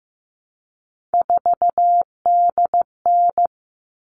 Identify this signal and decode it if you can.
Morse code, heard as a keyed tone, about 15 wpm, 700 Hz. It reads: 4DN
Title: All Cut Numbers - 3-Digit 15wpm